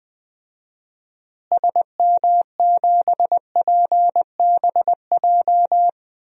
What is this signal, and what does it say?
Morse code, heard as a keyed tone, about 20 wpm, 700 Hz. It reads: SM7PBJ
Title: DXCC-100 Call Signs - 6 Characters - Repeated 2x - 20wpm